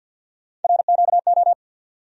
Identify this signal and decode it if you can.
Morse code, heard as a keyed tone, about 47 wpm, 700 Hz. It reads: RQO
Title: USA Call Sign Postfixes - 3 Letters 50wpm